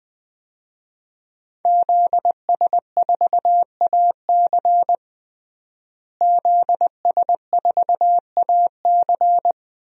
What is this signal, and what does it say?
Morse code, heard as a keyed tone, about 20 wpm, 700 Hz. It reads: ZS4AC ZS4AC
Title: DXCC-100 Call Signs - 5 Characters - Repeated 2x - 20wpm